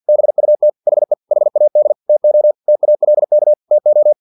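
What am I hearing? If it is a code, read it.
Morse code, heard as a keyed tone, about 50 wpm, 600 Hz. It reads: BUT HE HAD TO TALK TO
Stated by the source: Morse practice text from an English-language novel at 49 wpm.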